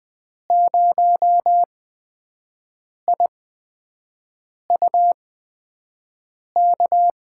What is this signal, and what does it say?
Morse code, heard as a keyed tone, about 20 wpm, spaced as for 5 wpm, 700 Hz. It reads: 0IUK
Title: USA Call Signs - 8x Character Spacing - 20wpm